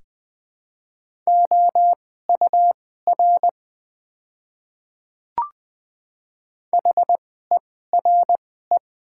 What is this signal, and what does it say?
Morse code, heard as a keyed tone, about 20 wpm, spaced as for 15 wpm, 700 Hz. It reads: OUR HERE
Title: 2x Character Spacing - Top 200 Words - 20wpm